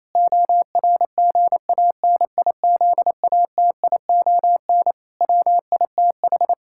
Morse code, heard as a keyed tone, about 28 wpm, 700 Hz. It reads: ORGANIZATION WITH
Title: Sentences from Top 1000 Words - Review Part 5 - 28wpm